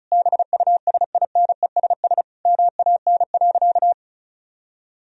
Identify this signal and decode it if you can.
Morse code, heard as a keyed tone, about 34 wpm, 700 Hz. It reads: BUSINESS MAN.
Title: Sentences from Top 1000 Words - Review Part 4 - 35wpm